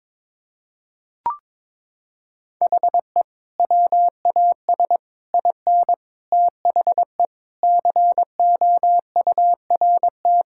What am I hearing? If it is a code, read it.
Morse code, heard as a keyed tone, about 22 wpm, 700 Hz. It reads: HE WAS IN THE COURT
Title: Sentences from Top 1600 Words 22wpm